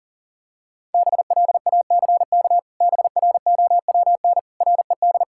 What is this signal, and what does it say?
Morse code, heard as a keyed tone, about 40 wpm, 700 Hz. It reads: BLACK BROWN RED
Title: Pangram Challenge v2 - 40wpm